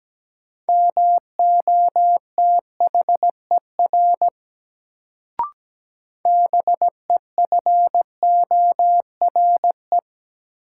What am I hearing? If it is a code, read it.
Morse code, heard as a keyed tone, about 17 wpm, 700 Hz. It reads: MOTHER BEFORE
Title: Sentences from Top 200 Words - New Word Drill - 17wpm